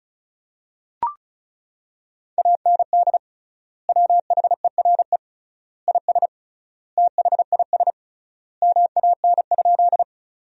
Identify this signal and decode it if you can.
Morse code, heard as a keyed tone, about 34 wpm, 700 Hz. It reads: AND WHERE IS THIS MAN?
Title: 3x Word-Spacing - Sentences from Top 200 Words - 35wpm